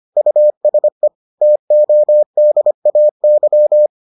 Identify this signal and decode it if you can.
Morse code, heard as a keyed tone, about 25 wpm, 600 Hz. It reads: USE TODAY